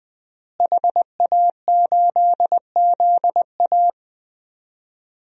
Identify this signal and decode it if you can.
Morse code, heard as a keyed tone, about 20 wpm, 700 Hz. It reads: HA8ZA
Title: DXCC-100 Call Signs - 5 Characters - Repeated 3x - 20wpm